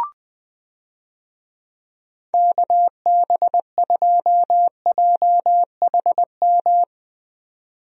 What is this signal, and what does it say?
Morse code, heard as a keyed tone, about 20 wpm, 700 Hz. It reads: KB2JHM
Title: USA Call Signs - 6 Characters - 20wpm